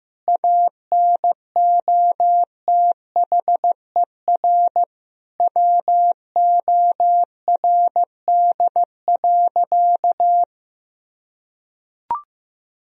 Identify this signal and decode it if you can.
Morse code, heard as a keyed tone, about 15 wpm, 700 Hz. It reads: ANOTHER WORD.
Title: Sentences from Top 1000 Words - Review Part 3 - 15wpm